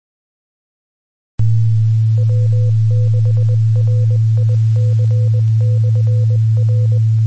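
Morse code, heard as a keyed tone, about 20 wpm, 500 Hz, with noise. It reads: W6RIC/R
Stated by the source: Morse ident recorded on 2 m at 146.850 MHz.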